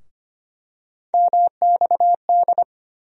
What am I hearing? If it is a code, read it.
Morse code, heard as a keyed tone, about 25 wpm, 700 Hz. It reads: MXD
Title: USA Call Sign Postfixes - 3 Letters 25wpm